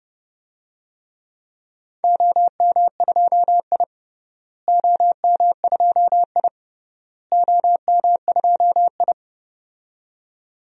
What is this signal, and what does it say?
Morse code, heard as a keyed tone, about 32 wpm, 700 Hz. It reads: OM2I OM2I OM2I
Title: DXCC-100 Call Signs - 4 Characters - Repeated 3x - 30wpm